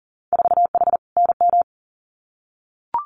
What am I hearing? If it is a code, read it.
Morse code, heard as a keyed tone, about 40 wpm, 700 Hz. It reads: 4H NM
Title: Winter Field Day Exchanges Encore 40wpm